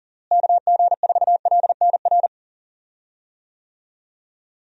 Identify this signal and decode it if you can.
Morse code, heard as a keyed tone, about 39 wpm, 700 Hz. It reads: KG4LNR
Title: USA Call Signs - 6 Characters - Repeated 3x - 40wpm